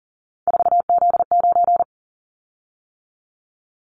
Morse code, heard as a keyed tone, about 38 wpm, 700 Hz. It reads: 4Z9
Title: Callsign Prefixes for Top 50 DXCC - 40wpm